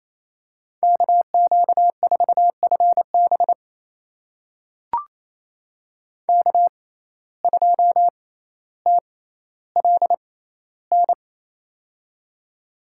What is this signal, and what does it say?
Morse code, heard as a keyed tone, about 28 wpm, spaced as for 14 wpm, 700 Hz. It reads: KQ4FB K2TLN
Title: USA Call Signs - 5 Characters - 6x Character Spacing - 28wpm